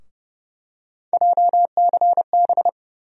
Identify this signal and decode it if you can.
Morse code, heard as a keyed tone, about 30 wpm, 700 Hz. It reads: JCB
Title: USA Call Sign Postfixes - 3 Letters 30wpm